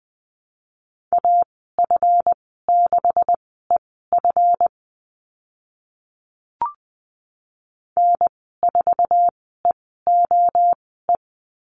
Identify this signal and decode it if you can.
Morse code, heard as a keyed tone, about 20 wpm, spaced as for 14 wpm, 700 Hz. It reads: AF6EF N4EOE
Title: 147 Lesson 24g - Letter y - Call Signs 20wpm